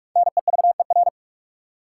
Morse code, heard as a keyed tone, about 41 wpm, 700 Hz. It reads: NEVER
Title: Five Letter Words - 45wpm